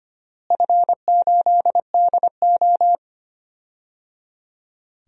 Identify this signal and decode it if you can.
Morse code, heard as a keyed tone, about 25 wpm, 700 Hz. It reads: F8DO
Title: DXCC-100 Call Signs - 4 Characters - Repeated 3x - 25wpm